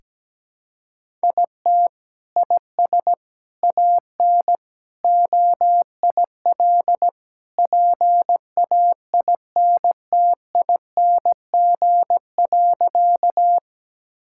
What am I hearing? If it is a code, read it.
Morse code, heard as a keyed tone, about 17 wpm, 700 Hz. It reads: IT IS AN OIL PAINTING.
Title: Sentences from Top 1000 Words - Review Part 6 - 17wpm